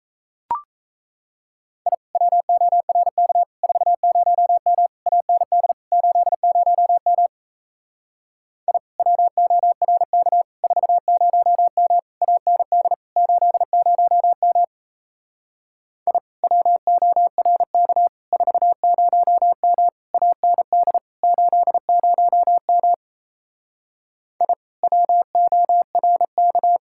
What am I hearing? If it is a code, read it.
Morse code, heard as a keyed tone, about 35 wpm, 700 Hz. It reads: I WORK 40M AND 80M I WORK 40M AND 80M I WORK 40M AND 80M I WORK
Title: Speed-Racing Common QSO Phrases - 28wpm